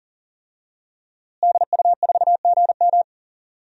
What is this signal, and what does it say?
Morse code, heard as a keyed tone, about 40 wpm, 700 Hz. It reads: DU4GM
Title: DXCC-100 Call Signs - 5 Characters - Repeated 3x - 40wpm